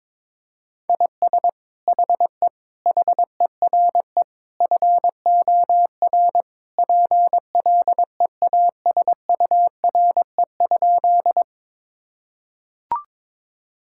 Morse code, heard as a keyed tone, about 22 wpm, 700 Hz. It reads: IS HE HERE FOR PLEASURE?
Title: Sentences from Top 1600 Words 22wpm